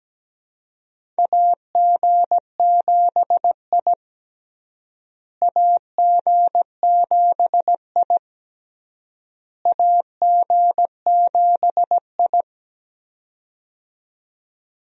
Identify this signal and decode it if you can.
Morse code, heard as a keyed tone, about 17 wpm, 700 Hz. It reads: AG7I AG7I AG7I
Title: USA Call Signs - 4 Characters - Repeated 3x - 17wpm